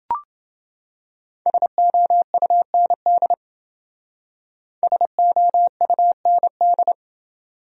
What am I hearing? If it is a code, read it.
Morse code, heard as a keyed tone, about 30 wpm, 700 Hz. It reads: SOUND SOUND